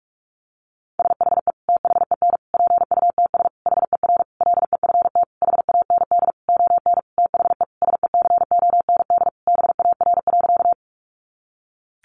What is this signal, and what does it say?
Morse code, heard as a keyed tone, about 44 wpm, 700 Hz. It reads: SHE THEN PUTS HER LEFT HAND ON THE SECOND BAR.
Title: Easy Phrases 45wpm